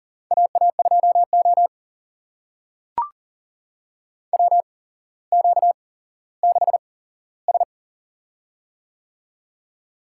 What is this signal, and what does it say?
Morse code, heard as a keyed tone, about 37 wpm, spaced as for 17 wpm, 700 Hz. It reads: AA2O WQ6S
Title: USA Call Signs - 4 Characters - 8x Character Spacing - 40wpm